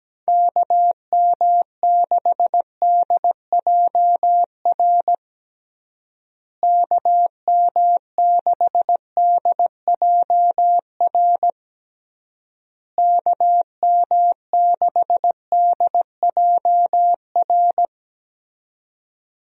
Morse code, heard as a keyed tone, about 17 wpm, 700 Hz. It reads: KM6DJR KM6DJR KM6DJR
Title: USA Call Signs - 6 Characters - Repeated 3x - 17wpm